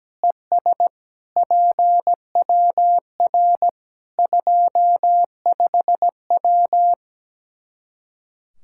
Morse code, heard as a keyed tone, about 17 wpm, 700 Hz. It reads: ES PWR 25W
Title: Practice QSO - Exchange 2 - Info - 17wpm